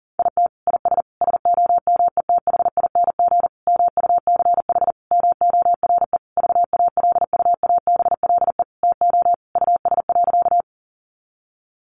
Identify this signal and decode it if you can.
Morse code, heard as a keyed tone, about 40 wpm, 700 Hz. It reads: IT IS SOMETHING MUCH MORE VALUABLE TO US.
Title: Sentences from Top 1100 Words 40wpm